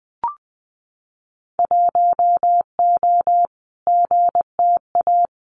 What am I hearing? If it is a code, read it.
Morse code, heard as a keyed tone, about 20 wpm, 700 Hz. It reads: 1O GTA
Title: Winter Field Day Exchanges 20wpm